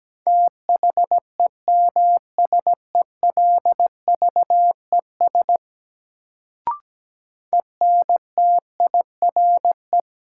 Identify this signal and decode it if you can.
Morse code, heard as a keyed tone, about 17 wpm, 700 Hz. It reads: THEMSELVES ENTIRE
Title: Sentences from Top 800 Words - New Word Drill - 17wpm